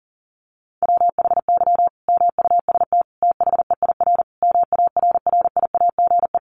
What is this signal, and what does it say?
Morse code, heard as a keyed tone, about 40 wpm, 700 Hz. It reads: WHY MUST THEIR MARRIAGE
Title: Sentences from Top 1000 Words 40wpm